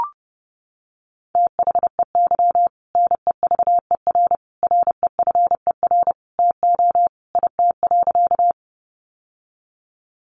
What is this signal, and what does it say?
Morse code, heard as a keyed tone, about 30 wpm, 700 Hz. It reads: THEY NEVER REFER TO IT.
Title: Sentences from Top 1000 Words 30wpm